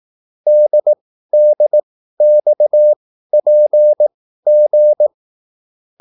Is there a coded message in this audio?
Yes: Morse code from 0.5 to 5.1 s, heard as a keyed tone, about 18 wpm, spaced as for 13 wpm, 600 Hz. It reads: DDXPG